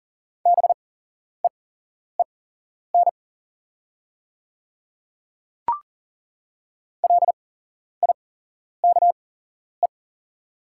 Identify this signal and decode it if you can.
Morse code, heard as a keyed tone, about 38 wpm, spaced as for 11 wpm, 700 Hz. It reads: BEEN LIKE
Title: 8x Character Spacing - Top 100 Words - 40wpm